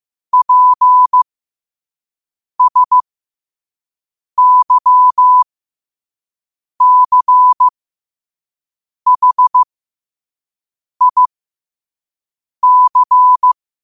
Morse code, heard as a keyed tone, about 15 wpm, spaced as for 5 wpm, 1000 Hz. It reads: PSYCHIC
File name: qotdfiles / 5wpm